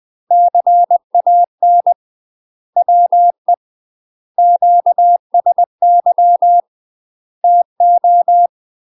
Morse code, heard as a keyed tone, about 20 wpm, 700 Hz. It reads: CAN WE QSY TO
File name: UK MASTER QSOs TRAINING / 20WPM / UK MASTER QSOs 0026